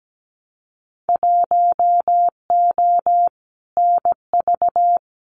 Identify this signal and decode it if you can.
Morse code, heard as a keyed tone, about 17 wpm, 700 Hz. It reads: 1O NV
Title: Winter Field Day Exchanges Encore 17wpm